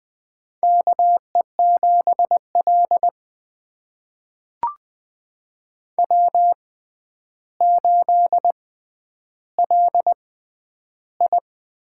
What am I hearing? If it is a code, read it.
Morse code, heard as a keyed tone, about 21 wpm, spaced as for 10 wpm, 700 Hz. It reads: KE7L W8LI
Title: USA Call Signs - 4 Characters - 6x Character Spacing - 20wpm